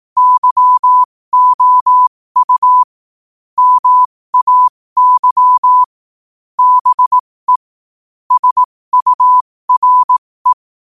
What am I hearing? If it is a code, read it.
Morse code, heard as a keyed tone, about 18 wpm, spaced as for 15 wpm, 1000 Hz. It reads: YOU MAY BE SURE
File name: qotdfiles / 15wpm